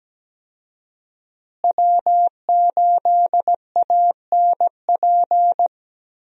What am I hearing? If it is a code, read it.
Morse code, heard as a keyed tone, about 17 wpm, 700 Hz. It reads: W8ANP
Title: USA Call Signs - 5 Characters - Repeated 2x - 17wpm